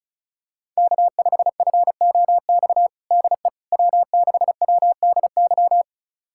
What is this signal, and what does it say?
Morse code, heard as a keyed tone, about 34 wpm, 700 Hz. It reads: K5FOX DE W6WDY
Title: USA Callsign DE Callsign - 5 Characters - 35wpm